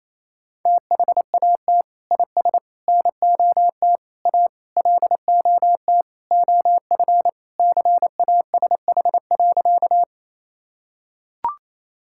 Morse code, heard as a keyed tone, about 28 wpm, 700 Hz. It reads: THAT IS NOT A LOT OF CASH.
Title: Sentences from Top 1000 Words - Review Part 5 - 28wpm